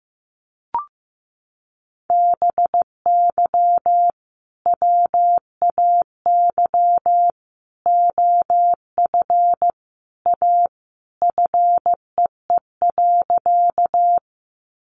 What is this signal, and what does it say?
Morse code, heard as a keyed tone, about 15 wpm, 700 Hz. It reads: BY WAY OF A FEE.
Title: Sentences from Top 1500 Words 15wpm